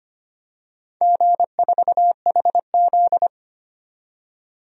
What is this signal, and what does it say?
Morse code, heard as a keyed tone, about 25 wpm, 700 Hz. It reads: G4HZ
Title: DXCC-100 Call Signs - 4 Characters - 25wpm